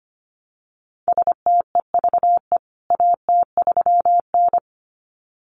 Morse code, heard as a keyed tone, about 25 wpm, 700 Hz. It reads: STEVE AT3N
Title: CWT Exchange 25wpm